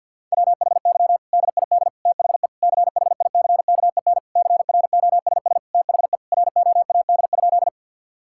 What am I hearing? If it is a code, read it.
Morse code, heard as a keyed tone, about 48 wpm, 700 Hz. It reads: WHY DID THE CHICKEN CROSS THE ROAD?
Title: Sentences from Top 1000 Words - Review Part 5 - 50wpm